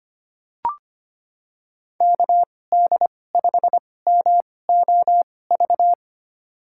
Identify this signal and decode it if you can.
Morse code, heard as a keyed tone, about 25 wpm, spaced as for 18 wpm, 700 Hz. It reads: KD5MOV
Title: USA Call Signs - 6 Characters - 2x Character Spacing - 25wpm